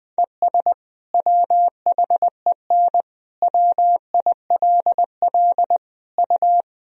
Morse code, heard as a keyed tone, about 20 wpm, 700 Hz. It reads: ES WHEN WILL U
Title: Sentences from Top 100 Words and Abbreviations 20wpm